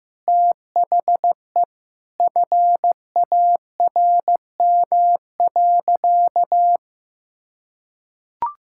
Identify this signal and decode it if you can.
Morse code, heard as a keyed tone, about 15 wpm, 700 Hz. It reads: THE FARM.
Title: Sentences from Top 1000 Words - Review Part 6 - 15wpm